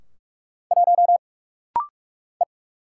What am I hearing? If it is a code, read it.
Morse code, heard as a keyed tone, about 45 wpm, 700 Hz. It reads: 1 E
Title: Single Letter-Number - Rapid-Fire 45wpm